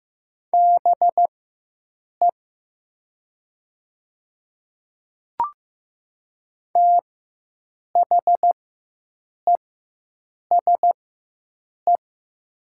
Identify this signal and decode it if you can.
Morse code, heard as a keyed tone, about 15 wpm, spaced as for 7 wpm, 700 Hz. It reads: BE THESE
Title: 4x Character Spacing - Top 100 Words - 15wpm